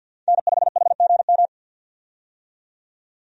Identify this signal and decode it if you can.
Morse code, heard as a keyed tone, about 53 wpm, 700 Hz. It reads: N5HGM
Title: USA Call Signs - 5 Characters - 50wpm